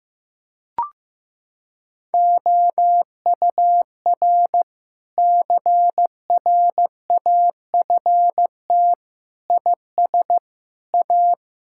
Morse code, heard as a keyed tone, about 16 wpm, 700 Hz. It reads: OUR CRAFT IS A
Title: Sentences from Top 1000 Words - Review Part 6 - 15wpm